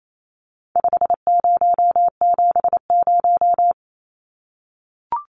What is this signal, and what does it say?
Morse code, heard as a keyed tone, about 28 wpm, 700 Hz. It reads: 5070
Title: Numbers - 4-Digit 28wpm